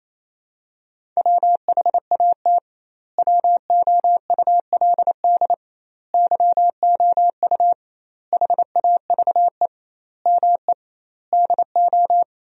Morse code, heard as a keyed tone, about 28 wpm, 700 Hz. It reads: WHAT WOULD YOU HAVE ME DO
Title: 2x Word-Spacing - Sentences from Top 300 Words - 28wpm